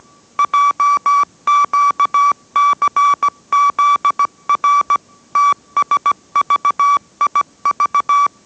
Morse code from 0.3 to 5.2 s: JQCZR